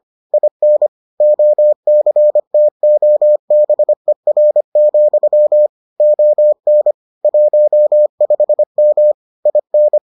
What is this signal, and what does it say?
Morse code, heard as a keyed tone, about 25 wpm, 600 Hz. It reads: IN OCTOBER, ON 15M IN